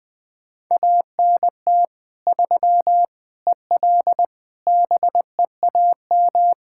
Morse code, heard as a keyed tone, about 21 wpm, 700 Hz. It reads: ANT 3 EL BEAM
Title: Practice QSO - Exchange 2 - Info - 20wpm